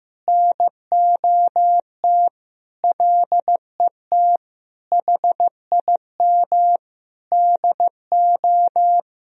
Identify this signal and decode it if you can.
Morse code, heard as a keyed tone, about 15 wpm, 700 Hz. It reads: NOT LET HIM DO
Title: Sentences from Top 1000 Words - Review Part 3 - 15wpm